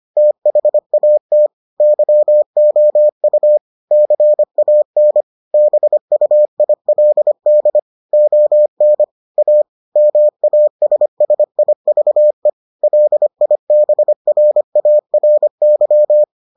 Morse code, heard as a keyed tone, about 25 wpm, 600 Hz. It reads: THAT YOU CAN BUILD ON A MASSIVE LIBRARY